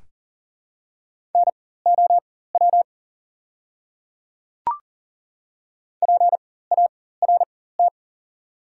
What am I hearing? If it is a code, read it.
Morse code, heard as a keyed tone, about 40 wpm, spaced as for 19 wpm, 700 Hz. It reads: NOW PART
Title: 4x Character Spacing - Top 300 Words - 40wpm